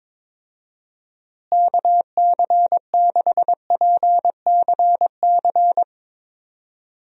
Morse code, heard as a keyed tone, about 22 wpm, 700 Hz. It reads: KC6PCC
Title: USA Call Signs - 6 Characters - Repeated 3x - 22wpm